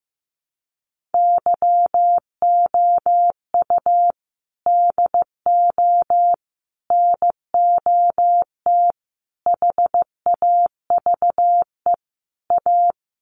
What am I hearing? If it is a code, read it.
Morse code, heard as a keyed tone, about 15 wpm, 700 Hz. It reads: YOU DO NOT HAVE A
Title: Sets of 5 Words 15wpm